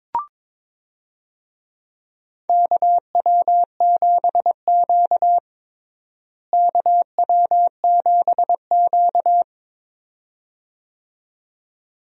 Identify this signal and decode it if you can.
Morse code, heard as a keyed tone, about 22 wpm, 700 Hz. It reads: KW7Q KW7Q
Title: USA Call Signs - 4 Characters - Repeated 2x - 22wpm